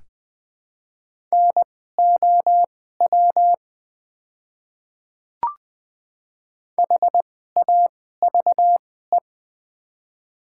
Morse code, heard as a keyed tone, about 20 wpm, spaced as for 15 wpm, 700 Hz. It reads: NOW HAVE
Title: 2x Character Spacing - Top 100 Words - 20wpm